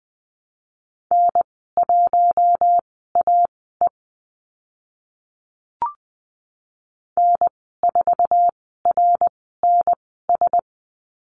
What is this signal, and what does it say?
Morse code, heard as a keyed tone, about 20 wpm, spaced as for 14 wpm, 700 Hz. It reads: N1AE N4RNS
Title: 062 Lesson 11g - Letter h - Call Signs 20wpm